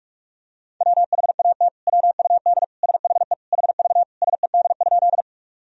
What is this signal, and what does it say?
Morse code, heard as a keyed tone, about 46 wpm, 700 Hz. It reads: WHAT WUD SHE HV SED?
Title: Sentences from Top 100 Words and Abbreviations 45wpm